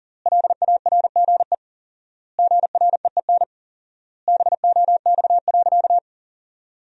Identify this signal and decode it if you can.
Morse code, heard as a keyed tone, about 40 wpm, 700 Hz. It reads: LARGE GREEN BOX.
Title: 4x Word-Spacing - Sentences from Top 500 Words - 40wpm